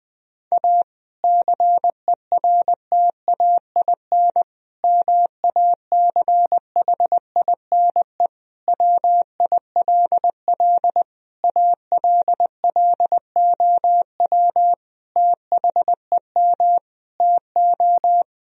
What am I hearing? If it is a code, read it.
Morse code, heard as a keyed tone, about 20 wpm, 700 Hz. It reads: A CERTAIN MACHINE WILL ALLOW THEM TO
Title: Sentences from Top 1000 Words - Review Part 2 - 20wpm